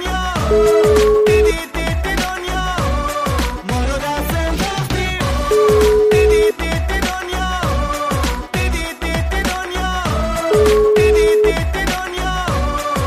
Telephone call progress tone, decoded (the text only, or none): ringback tone